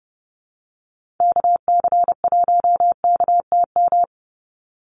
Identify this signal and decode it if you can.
Morse code, heard as a keyed tone, about 30 wpm, 700 Hz. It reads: KC1KTM
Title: USA Call Signs 30wpm